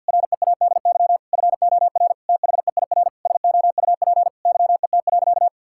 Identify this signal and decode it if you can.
Morse code, heard as a keyed tone, about 52 wpm, 700 Hz. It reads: READY FOR THEIR SOUP YET.
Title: Sentences from Top 1000 Words - Review Part 6 - 50wpm